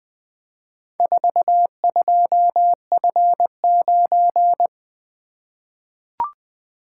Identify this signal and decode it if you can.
Morse code, heard as a keyed tone, about 20 wpm, 700 Hz. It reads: 42F9
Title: Four Letter-Number Combinations - 20wpm